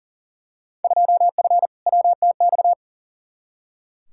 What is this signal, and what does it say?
Morse code, heard as a keyed tone, about 40 wpm, 700 Hz. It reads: 2F WTX
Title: Field Day Exchanges - 40wpm